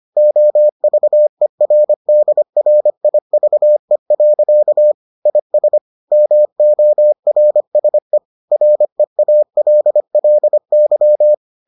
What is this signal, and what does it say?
Morse code, heard as a keyed tone, about 25 wpm, 600 Hz. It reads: OVERDRIVE. IS MORSE REALLY